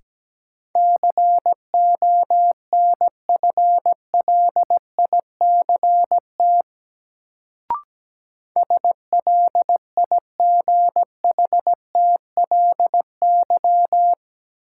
Morse code, heard as a keyed tone, about 17 wpm, 700 Hz. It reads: CONFLICT SLIGHTLY
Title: Sentences from Top 1300 Words - New Word Drill Alt - 17wpm